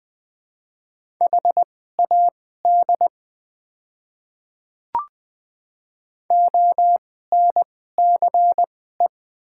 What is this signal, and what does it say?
Morse code, heard as a keyed tone, about 20 wpm, spaced as for 14 wpm, 700 Hz. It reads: HAD ONCE
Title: 2x Character Spacing - Top 300 Words - 20wpm